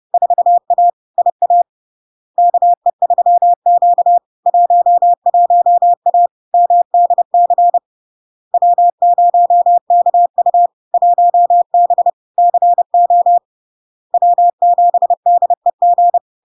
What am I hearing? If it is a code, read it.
Morse code, heard as a keyed tone, about 30 wpm, 700 Hz. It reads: 4A IA KE3Q 11A MDC W0KU 1B CO W7DEG
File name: LARK-FD-2022-Log-in-CW-30wpm-700-Hz